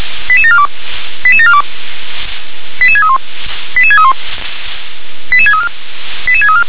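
Five-tone selective call (ZVEI), heard as quick runs of five tones, tones of about 70 ms. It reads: 88742 88742 88631 88631 88634 88634
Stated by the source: Received by SDR, ZVEI five-tone selective call.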